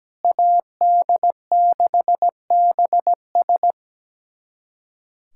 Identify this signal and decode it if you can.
Morse code, heard as a keyed tone, about 17 wpm, 700 Hz. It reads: AD6BS
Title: USA Callsign DE Callsign - 5 Characters - 17wpm